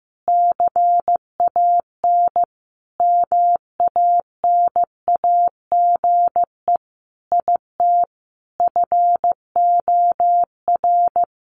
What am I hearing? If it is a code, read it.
Morse code, heard as a keyed tone, about 15 wpm, 700 Hz. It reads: CAN MANAGE IT FOR
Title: Sentences from Top 1400 Words 15wpm